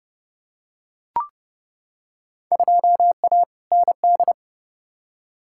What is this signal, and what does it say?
Morse code, heard as a keyed tone, about 30 wpm, 700 Hz. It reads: 2A ND
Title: Field Day Exchanges - 30wpm